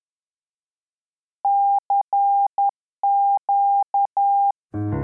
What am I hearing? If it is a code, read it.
Morse code, heard as a keyed tone, about 10 wpm, 800 Hz. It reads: CQ
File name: La-radiotelegrafia-Marittima-Puntata8